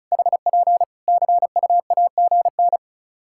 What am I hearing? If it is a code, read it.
Morse code, heard as a keyed tone, about 34 wpm, 700 Hz. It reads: HP CUAGN
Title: K1USN SST QSO Practice 35wpm